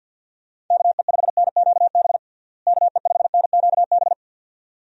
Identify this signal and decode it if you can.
Morse code, heard as a keyed tone, about 52 wpm, 700 Hz. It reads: KE5NQB KE5NQB
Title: USA Call Signs - 6 Characters - Repeated 2x - 50wpm